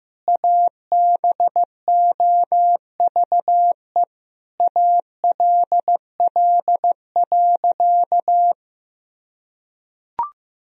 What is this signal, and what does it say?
Morse code, heard as a keyed tone, about 15 wpm, 700 Hz. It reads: ABOVE ALL.
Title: Sentences from Top 1000 Words - Review Part 2 - 15wpm